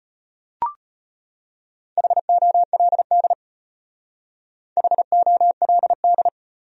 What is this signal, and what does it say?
Morse code, heard as a keyed tone, about 33 wpm, 700 Hz. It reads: HOLD HOLD